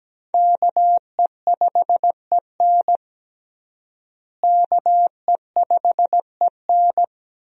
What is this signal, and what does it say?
Morse code, heard as a keyed tone, about 17 wpm, 700 Hz. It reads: KE5EN KE5EN
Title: USA Call Signs - 5 Characters - Repeated 2x - 17wpm